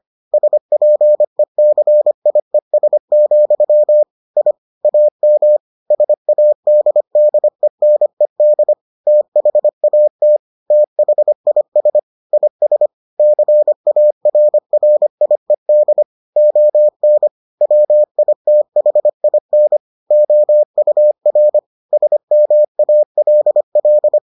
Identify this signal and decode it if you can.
Morse code, heard as a keyed tone, about 25 wpm, 600 Hz. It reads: SPECIES, I AM SADDENED THAT THIS IS CARRIED ON WITHIN OUR SMALL